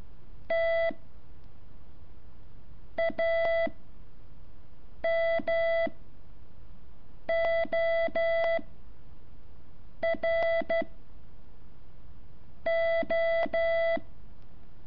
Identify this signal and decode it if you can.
Morse code, heard as a keyed tone, about 12 wpm, 660 Hz. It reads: T A M O R O
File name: curso / ejercicio 8